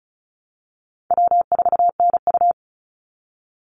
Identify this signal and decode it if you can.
Morse code, heard as a keyed tone, about 34 wpm, 700 Hz. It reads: W4NU
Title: Easy Call Signs Repeated 3x - 35wpm